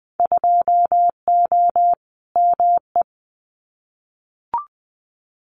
Morse code, heard as a keyed tone, about 20 wpm, 700 Hz. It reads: 2O ME
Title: Winter Field Day Exchanges Encore 20wpm